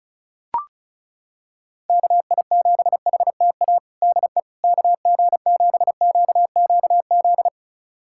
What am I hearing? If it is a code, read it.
Morse code, heard as a keyed tone, about 34 wpm, 700 Hz. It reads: KI7HTA DE KG7QQZ
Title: USA Callsign DE Callsign - 6 Characters - 35wpm